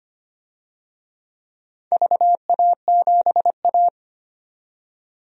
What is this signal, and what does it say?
Morse code, heard as a keyed tone, about 25 wpm, 700 Hz. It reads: VA7A
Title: DXCC-100 Call Signs - 4 Characters - 25wpm